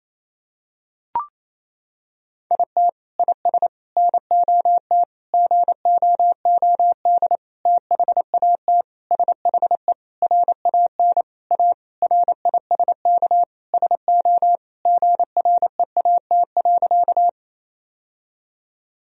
Morse code, heard as a keyed tone, about 28 wpm, 700 Hz. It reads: IT IS NOT GOOD THAT SHE RAN A RISK SO GREAT.
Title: Sentences from Top 1000 Words - Review Part 4 - 28wpm